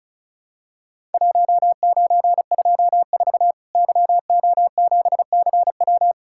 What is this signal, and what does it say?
Morse code, heard as a keyed tone, about 36 wpm, 700 Hz. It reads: 1924 YO7CW
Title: Worked All Europe DX Contest - QTC Exchange - 35wpm